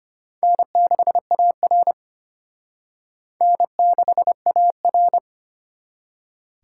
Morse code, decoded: N6AR N6AR

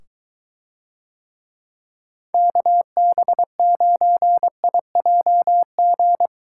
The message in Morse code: KB9IJG

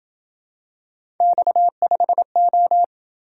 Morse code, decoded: X5O

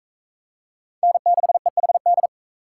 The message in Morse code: N6EHD

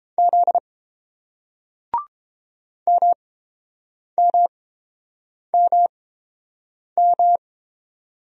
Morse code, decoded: Z MMMM

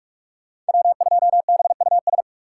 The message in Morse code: W2BUS

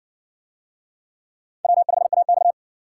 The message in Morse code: W5AX

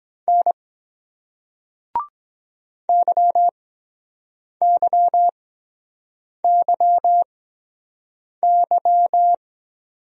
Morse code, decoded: N YYYY

